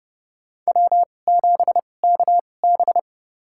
Morse code, heard as a keyed tone, about 30 wpm, spaced as for 22 wpm, 700 Hz. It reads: W7KB
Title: USA Call Signs - 4 Characters - 2x Character Spacing - 30wpm